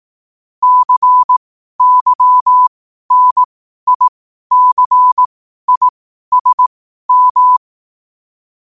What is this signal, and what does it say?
Morse code, heard as a keyed tone, about 18 wpm, spaced as for 12 wpm, 1000 Hz. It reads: CYNICISM
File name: qotdfiles / 12wpm